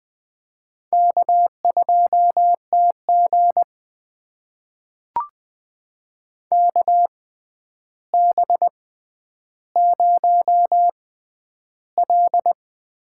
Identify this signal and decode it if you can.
Morse code, heard as a keyed tone, about 20 wpm, spaced as for 10 wpm, 700 Hz. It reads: K2TG KB0L
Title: USA Call Signs - 4 Characters - 6x Character Spacing - 20wpm